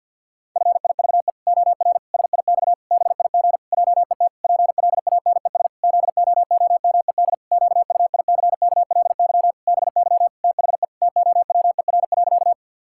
Morse code, heard as a keyed tone, about 46 wpm, 700 Hz. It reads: FIVE OR SIX BIG JET PLANES ZOOMED QUICKLY BY THE TOWER.
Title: Pangram Challenge v2 - 50wpm